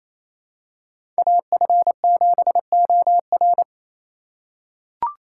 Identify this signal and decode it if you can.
Morse code, heard as a keyed tone, about 28 wpm, 700 Hz. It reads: AF7OR
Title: USA Call Signs - 5 Characters - 4x Character Spacing - 28wpm